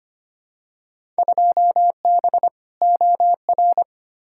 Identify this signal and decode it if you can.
Morse code, heard as a keyed tone, about 25 wpm, 700 Hz. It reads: 2B OR
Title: Field Day Exchanges Encore - 25wpm